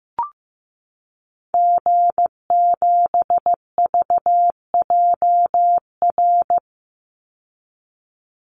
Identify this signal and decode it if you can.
Morse code, heard as a keyed tone, about 15 wpm, 700 Hz. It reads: G7VJR
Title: Call Signs Repeated 2x - 15wpm